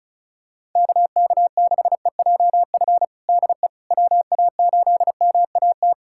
Morse code, heard as a keyed tone, about 36 wpm, 700 Hz. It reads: KK6EJF DE WA8MAT
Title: USA Callsign DE Callsign 35wpm